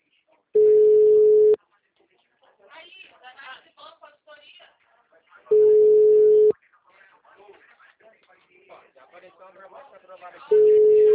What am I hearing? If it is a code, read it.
Ringback tone (Europe, 425 Hz), with speech.